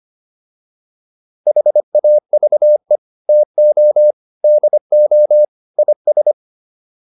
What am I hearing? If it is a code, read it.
Morse code, heard as a keyed tone, about 25 wpm, 600 Hz. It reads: HAVE TO DO IS